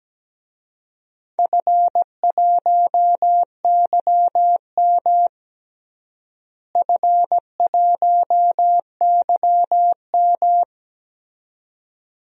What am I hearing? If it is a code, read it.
Morse code, heard as a keyed tone, about 17 wpm, 700 Hz. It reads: F1YM F1YM
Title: DXCC-100 Call Signs - 4 Characters - Repeated 2x - 17wpm